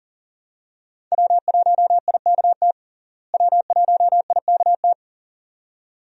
Morse code, heard as a keyed tone, about 37 wpm, 700 Hz. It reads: W1IKT W1IKT
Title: USA Call Signs - 5 Characters - Repeated 2x - 40wpm